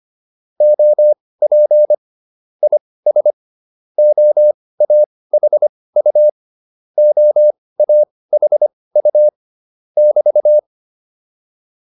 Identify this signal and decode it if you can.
Morse code, heard as a keyed tone, about 25 wpm, spaced as for 18 wpm, 600 Hz. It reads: OP IS OAHU OAHU <BT>
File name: QSO104 18